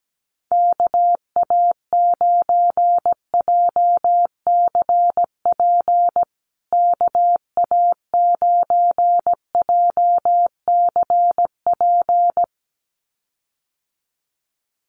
Morse code, heard as a keyed tone, about 17 wpm, 700 Hz. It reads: KA9JCP KA9JCP
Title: USA Call Signs Repeated 2x 17wpm